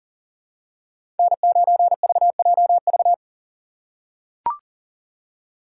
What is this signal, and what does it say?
Morse code, heard as a keyed tone, about 40 wpm, 700 Hz. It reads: N9VJV
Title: USA Call Signs - 3x Character Spacing - 40wpm